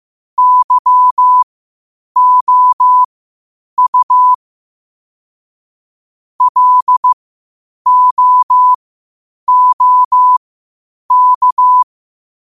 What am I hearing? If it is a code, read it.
Morse code, heard as a keyed tone, about 15 wpm, spaced as for 8 wpm, 1000 Hz. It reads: YOU LOOK
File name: qotdfiles / 8wpm